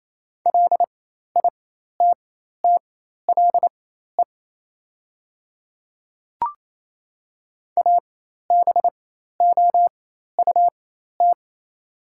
Morse code, heard as a keyed tone, about 28 wpm, spaced as for 13 wpm, 700 Hz. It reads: LITTLE ABOUT
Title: 4x Character Spacing - Top 200 Words - 28wpm